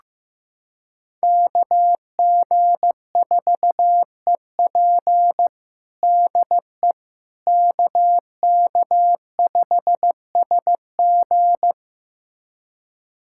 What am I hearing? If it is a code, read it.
Morse code, heard as a keyed tone, about 16 wpm, 700 Hz. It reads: KG4EP DE KK5SG
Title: USA Callsign DE Callsign - 5 Characters - 15wpm